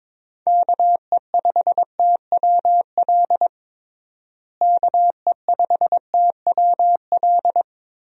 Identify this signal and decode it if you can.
Morse code, heard as a keyed tone, about 22 wpm, 700 Hz. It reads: KE5TWL KE5TWL
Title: USA Call Signs - 6 Characters - Repeated 3x - 22wpm